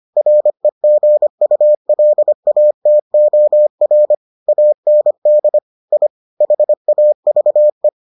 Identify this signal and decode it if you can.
Morse code, heard as a keyed tone, about 25 wpm, 600 Hz. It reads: REGULATOR AND I HAVE